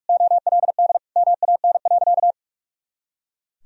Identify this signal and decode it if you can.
Morse code, heard as a keyed tone, about 42 wpm, 700 Hz. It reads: OLD MAN.
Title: Sentences from Top 200 Words and Abbreviations 45wpm